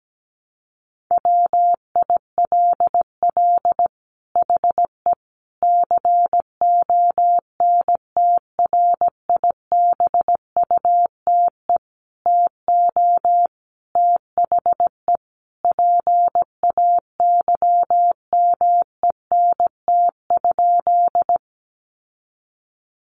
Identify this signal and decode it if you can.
Morse code, heard as a keyed tone, about 17 wpm, 700 Hz. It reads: WILL HE CONTRIBUTE TO THE PAYMENT?
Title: Sentences from Top 1400 Words 17wpm